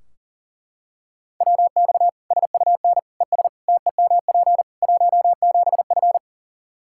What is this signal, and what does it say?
Morse code, heard as a keyed tone, about 37 wpm, 700 Hz. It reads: WX SUN ES TEMP 17F
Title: Practice QSO - Exchange 2 - Info - 40wpm